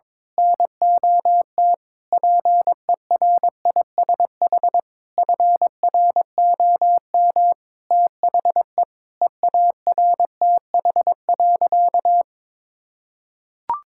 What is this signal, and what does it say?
Morse code, heard as a keyed tone, about 22 wpm, 700 Hz. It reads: NOT PERISH FROM THE EARTH.
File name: Sentences with 9-Word Phrases - 22wpm